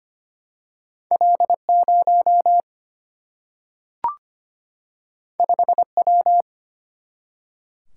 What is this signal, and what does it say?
Morse code, heard as a keyed tone, about 25 wpm, 700 Hz. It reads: L0 5W
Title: Single-Letter Single-Number 25wpm